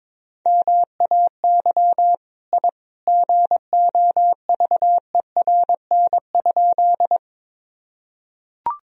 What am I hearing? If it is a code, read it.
Morse code, heard as a keyed tone, about 21 wpm, 700 Hz. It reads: MAY I GOVERN?
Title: Sentences from Top 1000 Words - Review Part 2 - 22wpm